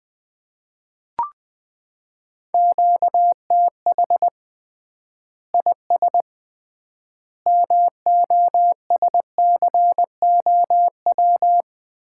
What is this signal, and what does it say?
Morse code, heard as a keyed tone, about 21 wpm, 700 Hz. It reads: QTH IS MOSCOW
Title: Common QSO Phrases - 3x Word-Spacing - 20wpm